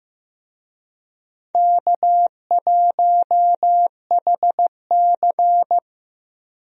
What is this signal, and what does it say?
Morse code, heard as a keyed tone, about 15 wpm, 700 Hz. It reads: K1HC
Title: USA Call Signs - 4 Characters - Repeated 2x - 15wpm